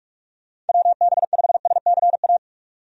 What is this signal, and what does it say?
Morse code, heard as a keyed tone, about 46 wpm, 700 Hz. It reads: WB5SCA
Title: USA Call Signs - 6 Characters - 45wpm